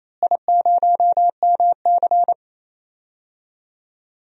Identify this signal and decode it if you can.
Morse code, heard as a keyed tone, about 28 wpm, 700 Hz. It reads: I0MC
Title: DXCC-100 Call Signs - 4 Characters - Repeated 3x - 28wpm